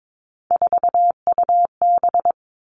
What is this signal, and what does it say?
Morse code, heard as a keyed tone, about 22 wpm, 700 Hz. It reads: 4UB
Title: Three Challenging Letter-Number Combinations 22wpm